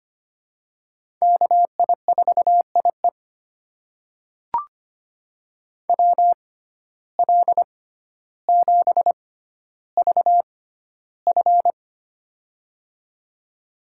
Morse code, heard as a keyed tone, about 25 wpm, spaced as for 13 wpm, 690 Hz. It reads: KI4IE WL7VF
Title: USA Call Signs - 5 Characters - 6x Character Spacing - 25wpm